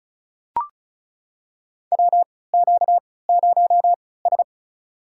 Morse code, heard as a keyed tone, about 36 wpm, spaced as for 20 wpm, 700 Hz. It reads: WQ0S
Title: USA Call Signs - 4 Characters - 3x Character Spacing - 35wpm